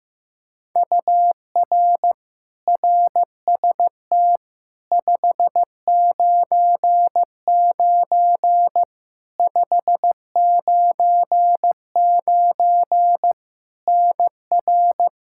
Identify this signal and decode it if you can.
Morse code, heard as a keyed tone, about 15 wpm, 700 Hz. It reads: UR RST 599 599 NR
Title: Practice QSO - Exchange 1 - Beginning - 15wpm